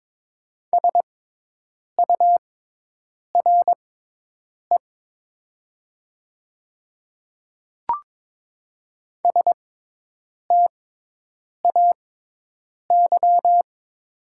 Morse code, heard as a keyed tone, about 22 wpm, spaced as for 8 wpm, 700 Hz. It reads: SURE STAY
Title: 6x Character Spacing - Top 500 Words - 22wpm